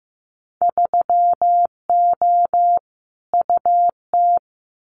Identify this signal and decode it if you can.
Morse code, heard as a keyed tone, about 15 wpm, 700 Hz. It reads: 3O UT
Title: Winter Field Day Exchanges 15wpm